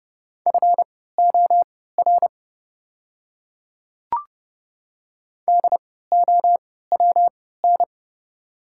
Morse code, heard as a keyed tone, about 30 wpm, spaced as for 17 wpm, 690 Hz. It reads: FOR DOWN
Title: 3x Character Spacing - Top 500 Words - 30wpm